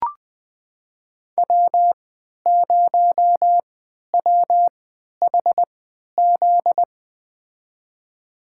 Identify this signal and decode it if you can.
Morse code, heard as a keyed tone, about 20 wpm, spaced as for 11 wpm, 700 Hz. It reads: W0WHZ
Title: USA Call Signs - 5 Characters - 3x Character Spacing - 20wpm